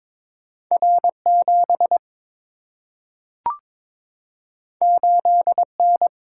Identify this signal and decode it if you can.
Morse code, heard as a keyed tone, about 22 wpm, 700 Hz. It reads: R 7 8 N
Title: Single-Letter Single-Number 22wpm